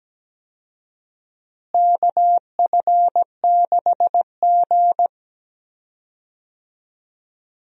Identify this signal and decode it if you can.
Morse code, heard as a keyed tone, about 17 wpm, 700 Hz. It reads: KF6G
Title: USA Call Signs - 4 Characters - Repeated 3x - 17wpm